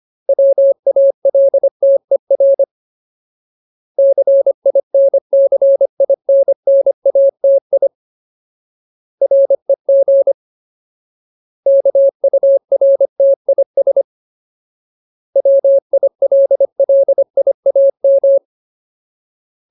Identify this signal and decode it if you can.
Morse code, heard as a keyed tone, about 25 wpm, 550 Hz. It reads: WALTER CINCINNATI REG KURTIS WILLIAM
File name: Ultimate Word List / 25WPM Ultimate